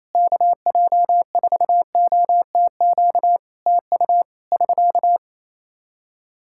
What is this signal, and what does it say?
Morse code, heard as a keyed tone, about 28 wpm, 700 Hz. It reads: KJ4OTQ TU <SK>
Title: Practice QSO - Exchange 4 - Ending - 28wpm